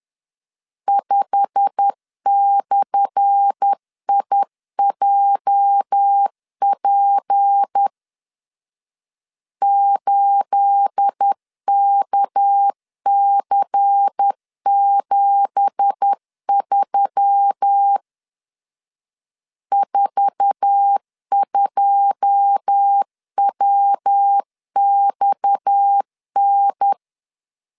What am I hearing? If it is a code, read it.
Morse code, heard as a keyed tone, about 10 wpm, 800 Hz. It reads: 5/IJP 8KC73 42WXN
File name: LEZIONI / 10 SIGNS / 10A SLASH 10wpm 2808